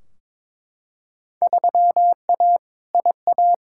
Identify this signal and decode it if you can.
Morse code, heard as a keyed tone, about 22 wpm, 700 Hz. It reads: 3A IA
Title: Field Day Exchanges Encore - 22wpm